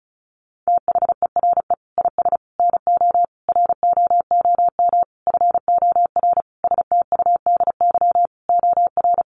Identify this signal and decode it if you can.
Morse code, heard as a keyed tone, about 34 wpm, 700 Hz. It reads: THERE IS NO ROOM FOR STUDY OR
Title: Sentences from Top 1200 Words 35wpm